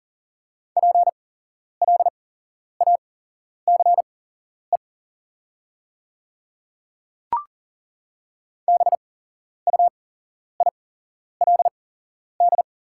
Morse code, heard as a keyed tone, about 40 wpm, spaced as for 11 wpm, 690 Hz. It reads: PLACE BUILD
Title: 8x Character Spacing - Top 300 Words - 40wpm